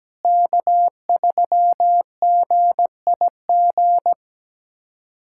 Morse code, heard as a keyed tone, about 17 wpm, 700 Hz. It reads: K3GIG
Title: USA Callsign DE Callsign - 5 Characters - 17wpm